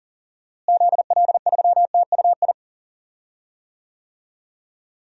Morse code, heard as a keyed tone, about 40 wpm, 700 Hz. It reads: ZL3TUI
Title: DXCC-100 Call Signs - 6 Characters - Repeated 2x - 40wpm